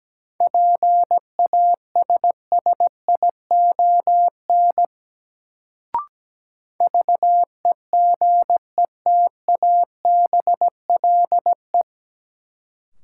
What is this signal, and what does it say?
Morse code, heard as a keyed tone, about 17 wpm, 700 Hz. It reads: PASSION VEGETABLE
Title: Sentences from Top 1300 Words - New Word Drill - 17wpm